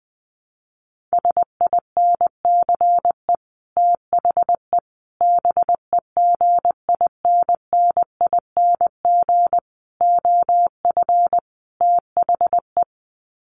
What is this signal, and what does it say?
Morse code, heard as a keyed tone, about 20 wpm, 700 Hz. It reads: SINCE THE BEGINNING OF THE
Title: Sets of 5 Words Encore2 20wpm